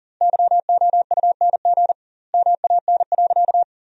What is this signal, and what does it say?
Morse code, heard as a keyed tone, about 40 wpm, 700 Hz. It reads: YOUNG MAN.
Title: 2x Word-Spacing - Sentences from Top 400 Words - 40wpm